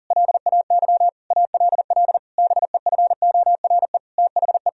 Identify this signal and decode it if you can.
Morse code, heard as a keyed tone, about 40 wpm, 700 Hz. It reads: LAY ALL BEFORE THE
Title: Sentences from Top 1000 Words - Review Part 2 - 40wpm